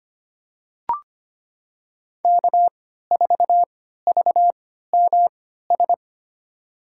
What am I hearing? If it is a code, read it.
Morse code, heard as a keyed tone, about 25 wpm, spaced as for 14 wpm, 700 Hz. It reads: K4VMS
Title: USA Call Signs - 3x Character Spacing - 25wpm